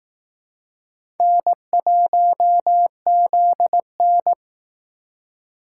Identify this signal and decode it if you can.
Morse code, heard as a keyed tone, about 19 wpm, 700 Hz. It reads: N1ZN